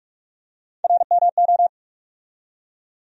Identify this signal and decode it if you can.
Morse code, heard as a keyed tone, about 48 wpm, 700 Hz. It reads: RMO
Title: USA Call Sign Postfixes - 3 Letters 45wpm